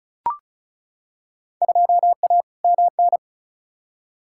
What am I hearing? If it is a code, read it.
Morse code, heard as a keyed tone, about 34 wpm, 700 Hz. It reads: 2A MN
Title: Field Day Exchanges - 35wpm